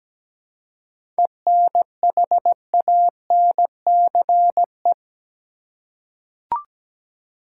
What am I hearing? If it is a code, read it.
Morse code, heard as a keyed tone, about 17 wpm, 700 Hz. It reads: ENHANCE
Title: Seven Letter Words - 17wpm